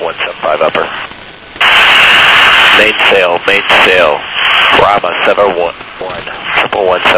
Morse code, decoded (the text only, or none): none